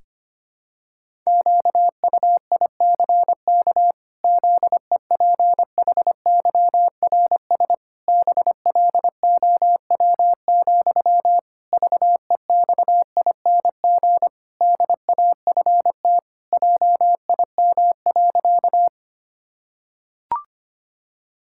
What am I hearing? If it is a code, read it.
Morse code, heard as a keyed tone, about 25 wpm, 700 Hz. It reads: QUICK ZEPHYRS BLOW, VEXING DAFT JIM.
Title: Pangram Challenge v2 - 25wpm